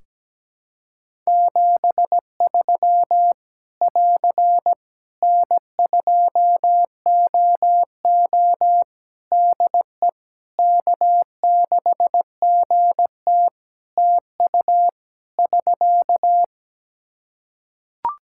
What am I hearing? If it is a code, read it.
Morse code, heard as a keyed tone, about 17 wpm, 700 Hz. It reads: 73 <AR> N2OO DE K6GT TU <SK>
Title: Practice QSO - Exchange 4 - Ending - 17wpm